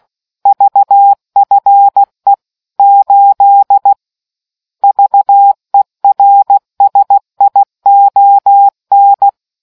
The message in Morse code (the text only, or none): VFE 8 VERSION